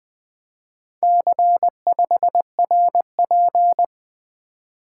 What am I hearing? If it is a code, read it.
Morse code, heard as a keyed tone, about 20 wpm, 700 Hz. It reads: C5RP
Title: Four Letter-Number Combinations - 20wpm